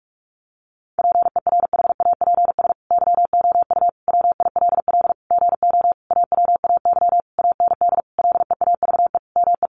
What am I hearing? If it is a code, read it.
Morse code, heard as a keyed tone, about 44 wpm, 700 Hz. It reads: PERHAPS YOU WILL GO AWAY AND LEAVE ME
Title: Sentences from Top 1300 Words 45wpm